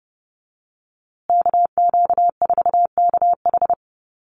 Morse code, heard as a keyed tone, about 30 wpm, 700 Hz. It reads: KQ4KH
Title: USA Call Signs Repeated 2x 30wpm